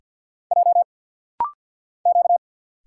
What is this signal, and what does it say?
Morse code, heard as a keyed tone, about 50 wpm, 700 Hz. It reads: J Q
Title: Single Letters - Rapid-Fire 50wpm